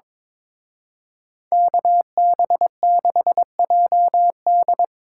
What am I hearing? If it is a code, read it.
Morse code, heard as a keyed tone, about 25 wpm, 700 Hz. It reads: KB6JD
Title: Speed Racing - USA 2x2 Call Signs 22wpm